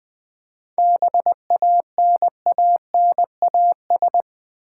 Morse code, heard as a keyed tone, about 20 wpm, 700 Hz. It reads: BANANAS
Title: Easy Words with Hints - 20wpm